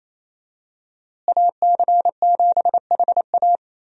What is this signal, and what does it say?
Morse code, heard as a keyed tone, about 28 wpm, 700 Hz. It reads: AC7HA
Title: USA Call Signs - 5 Characters - Repeated 2x - 28wpm